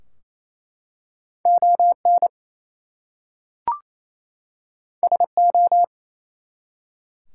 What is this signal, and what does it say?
Morse code, heard as a keyed tone, about 27 wpm, 700 Hz. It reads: ON SO
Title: Two Letter Words - 28wpm